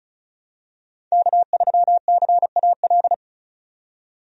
Morse code, heard as a keyed tone, about 36 wpm, 700 Hz. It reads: K3CAL
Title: USA Call Signs - 5 Characters - 35wpm